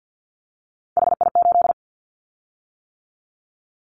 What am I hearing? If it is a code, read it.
Morse code, heard as a keyed tone, about 50 wpm, 700 Hz. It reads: HI8
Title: Callsign Prefixes for Top 100 DXCC - 50wpm